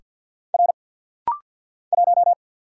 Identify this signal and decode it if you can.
Morse code, heard as a keyed tone, about 50 wpm, 700 Hz. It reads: R 1
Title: Single Letter-Number Alt - Rapid-Fire 50wpm